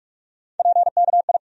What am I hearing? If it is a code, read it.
Morse code, heard as a keyed tone, about 46 wpm, 700 Hz. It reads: PKI
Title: USA Call Sign Postfixes - 3 Letters 45wpm